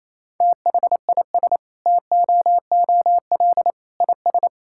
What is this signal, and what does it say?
Morse code, heard as a keyed tone, about 27 wpm, 700 Hz. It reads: THIS TOOL IS
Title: Sentences from Top 1000 Words - Review Part 6 - 28wpm